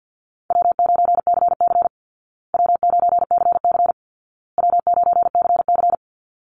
Morse code, heard as a keyed tone, about 50 wpm, 700 Hz. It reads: W9CC W9CC W9CC
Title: Easy Call Signs Repeated 3x - 50wpm